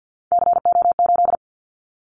Morse code, heard as a keyed tone, about 50 wpm, 700 Hz. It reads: CO8
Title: Callsign Prefixes for Top 100 DXCC - 50wpm